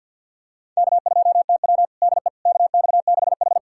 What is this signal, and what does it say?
Morse code, heard as a keyed tone, about 51 wpm, 700 Hz. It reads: K2TW DE KX6H
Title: USA Callsign DE Callsign - 4 Characters - 50wpm